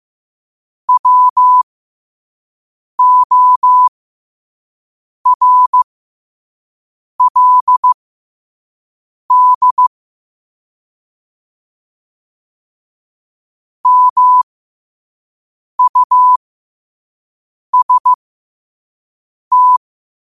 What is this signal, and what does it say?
Morse code, heard as a keyed tone, about 15 wpm, spaced as for 5 wpm, 1000 Hz. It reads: WORLD MUST
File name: qotdfiles / 5wpm